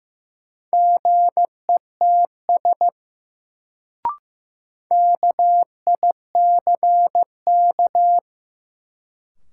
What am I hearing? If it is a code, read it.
Morse code, heard as a keyed tone, about 15 wpm, 700 Hz. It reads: GETS KICK
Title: Sentences from Top 1300 Words - New Word Drill Alt - 15wpm